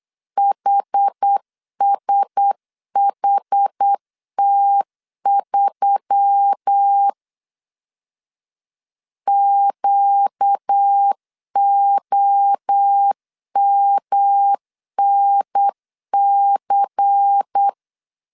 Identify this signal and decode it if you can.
Morse code, heard as a keyed tone, about 9 wpm, 800 Hz. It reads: HSHT3 QOMNC
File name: CW / DET / 1 6 1500C S A 8wpm 5203